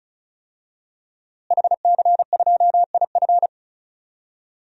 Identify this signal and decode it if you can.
Morse code, heard as a keyed tone, about 36 wpm, 700 Hz. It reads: HC2IF
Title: DXCC-100 Call Signs - 5 Characters - Repeated 2x - 35wpm